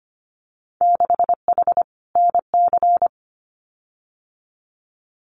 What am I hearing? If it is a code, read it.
Morse code, heard as a keyed tone, about 25 wpm, 700 Hz. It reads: 6H NC
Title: Winter Field Day Exchanges Encore 25wpm